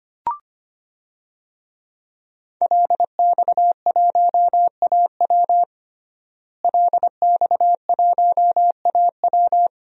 Morse code, heard as a keyed tone, about 25 wpm, 700 Hz. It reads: LX1AW LX1AW
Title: DXCC-100 Call Signs - 5 Characters - Repeated 3x - 25wpm